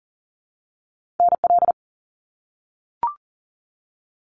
Morse code, heard as a keyed tone, about 40 wpm, 700 Hz. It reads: NL
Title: Canadian Provinces and Territories - Repeated 3x - 40wpm